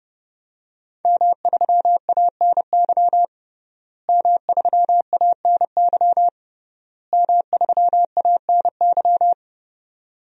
Morse code, heard as a keyed tone, about 32 wpm, 700 Hz. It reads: M3ANY M3ANY M3ANY
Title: DXCC-100 Call Signs - 5 Characters - Repeated 3x - 30wpm